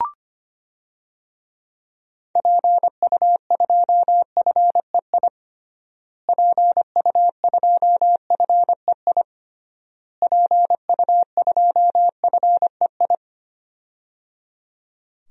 Morse code, heard as a keyed tone, about 25 wpm, 700 Hz. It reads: PU2FEI PU2FEI PU2FEI
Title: DXCC-100 Call Signs - 6 Characters - Repeated 3x - 25wpm